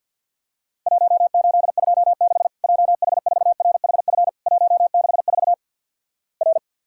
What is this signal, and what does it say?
Morse code, heard as a keyed tone, about 54 wpm, 700 Hz. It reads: 1826 JH4RHF 164 R
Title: Worked All Europe DX Contest - QTC Exchange - 50wpm